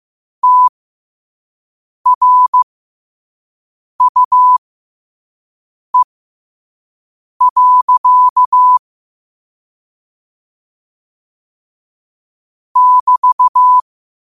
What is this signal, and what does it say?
Morse code, heard as a keyed tone, about 15 wpm, spaced as for 5 wpm, 1000 Hz. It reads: TRUE. <BT>
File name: qotdfiles / 5wpm